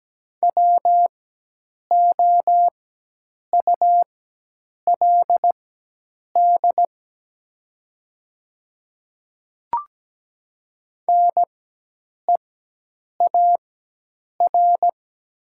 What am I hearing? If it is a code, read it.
Morse code, heard as a keyed tone, about 17 wpm, spaced as for 8 wpm, 700 Hz. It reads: WOULD NEAR
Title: 4x Character Spacing - Top 200 Words - 17wpm